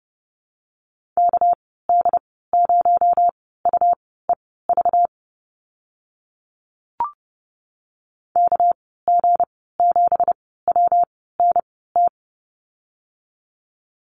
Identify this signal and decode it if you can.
Morse code, heard as a keyed tone, about 30 wpm, spaced as for 17 wpm, 700 Hz. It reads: KD0UEV KG7WNT
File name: Lesson 37g - Letter x - Call Signs 30wpm